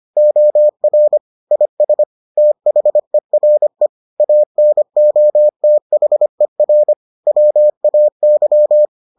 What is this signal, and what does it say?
Morse code, heard as a keyed tone, about 25 wpm, 600 Hz. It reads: OR IS THERE ANOTHER WAY